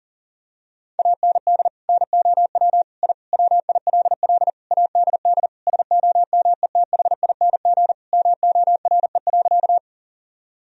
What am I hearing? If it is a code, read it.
Morse code, heard as a keyed tone, about 40 wpm, 700 Hz. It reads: AND NOW I WILL ADD SOMETHING MORE.
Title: Sentences from Top 1000 Words - Review Part 1 - 40wpm